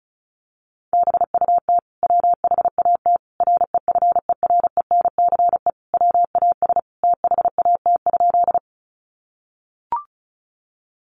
Morse code, heard as a keyed tone, about 34 wpm, 700 Hz. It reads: BUT WHAT REFERENCE WAS THAT?
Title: Sentences from Top 1200 Words 35wpm